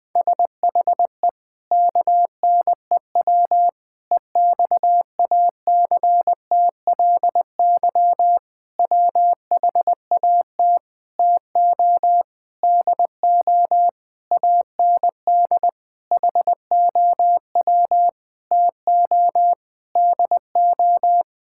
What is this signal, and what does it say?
Morse code, heard as a keyed tone, about 21 wpm, 700 Hz. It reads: SHE KNEW EXACTLY WHAT TO DO AND HOW TO DO
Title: Sentences with 8-Word Phrases - Spoken Beginning - 20wpm